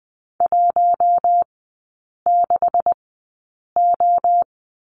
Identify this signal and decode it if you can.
Morse code, heard as a keyed tone, about 20 wpm, spaced as for 8 wpm, 700 Hz. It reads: 16O
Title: 184 Lesson 30e - Letter b - Three Characters 20wpm